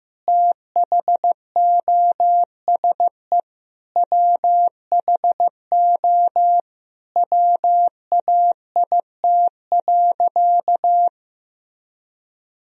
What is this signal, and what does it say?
Morse code, heard as a keyed tone, about 15 wpm, 700 Hz. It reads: THOSE WHO WAIT.
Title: Sentences from Top 1000 Words - Review Part 2 - 15wpm